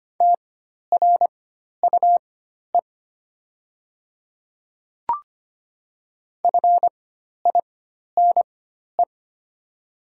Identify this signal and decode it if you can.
Morse code, heard as a keyed tone, about 25 wpm, spaced as for 12 wpm, 700 Hz. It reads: TRUE FINE
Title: 4x Character Spacing - Top 500 Words - 25wpm